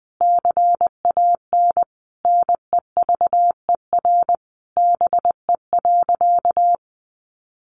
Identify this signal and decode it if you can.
Morse code, heard as a keyed tone, about 20 wpm, 700 Hz. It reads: CAN NEVER BE.
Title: Sentences from Top 300 Words 20wpm